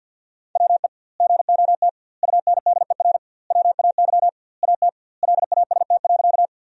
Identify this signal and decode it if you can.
Morse code, heard as a keyed tone, about 50 wpm, 700 Hz. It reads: WE GOT UNDER WAY AT LAST.
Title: 2x Word-Spacing - Sentences from Top 300 Words - 50wpm